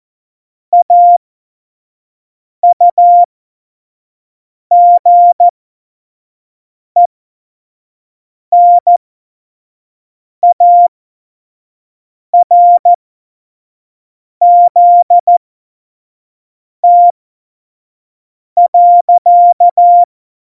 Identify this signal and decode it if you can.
Morse code, heard as a keyed tone, about 14 wpm, spaced as for 5 wpm, 700 Hz. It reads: AUGENARZT.
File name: Morsekurs / Witze / Witz 14. November 2017, 25 F70